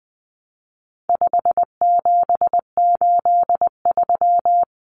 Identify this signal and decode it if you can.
Morse code, heard as a keyed tone, about 20 wpm, 700 Hz. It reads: 5783
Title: Numbers - 4-Digit 20wpm